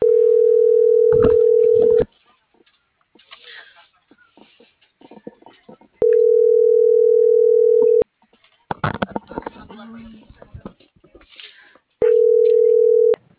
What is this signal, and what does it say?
Ringback tone (North America), 440 + 480 Hz.